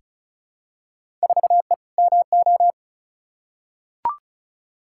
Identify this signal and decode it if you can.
Morse code, heard as a keyed tone, about 34 wpm, 700 Hz. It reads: 4E MO
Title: Field Day Exchanges Encore - 35wpm